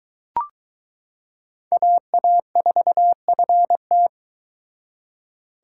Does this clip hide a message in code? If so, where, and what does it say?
Yes: Morse code from 1.7 to 4.1 s, heard as a keyed tone, about 22 wpm, 700 Hz. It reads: AA4FT